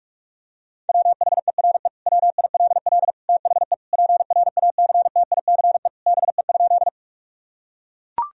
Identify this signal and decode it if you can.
Morse code, heard as a keyed tone, about 42 wpm, 700 Hz. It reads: WHERE WILL THE PRACTICE BE?
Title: Sentences from Top 1000 Words - Review Part 4 - 45wpm